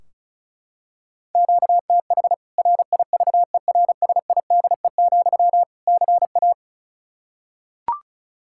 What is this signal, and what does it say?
Morse code, heard as a keyed tone, about 34 wpm, 700 Hz. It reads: QTH RIVERSIDE, CA
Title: Practice QSO - Exchange 1 - Beginning - 35wpm